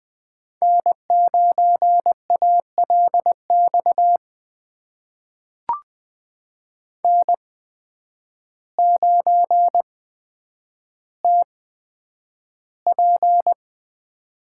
Morse code, heard as a keyed tone, about 20 wpm, spaced as for 9 wpm, 700 Hz. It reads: N9ALX N9TP
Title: USA Call Signs - 5 Characters - 8x Character Spacing - 20wpm